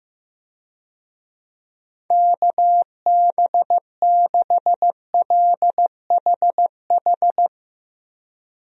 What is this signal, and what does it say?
Morse code, heard as a keyed tone, about 15 wpm, 700 Hz. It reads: KB6LHH
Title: USA Call Signs - 6 Characters - 15wpm